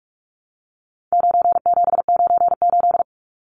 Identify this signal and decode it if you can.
Morse code, heard as a keyed tone, about 44 wpm, 700 Hz. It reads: 9798
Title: Numbers - 4-Digit 45wpm